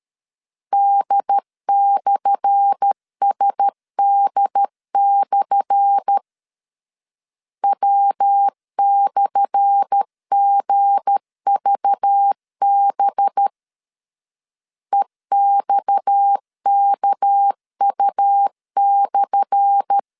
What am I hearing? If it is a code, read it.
Morse code, heard as a keyed tone, about 13 wpm, 800 Hz. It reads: D/SD/ W/GVB EXKU/
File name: LEZIONI / 10 SIGNS / 10A SLASH 12wpm 2314